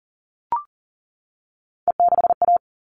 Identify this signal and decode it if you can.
Morse code, heard as a keyed tone, about 40 wpm, 700 Hz. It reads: E6A
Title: All Cut Numbers - 3-Digit 40wpm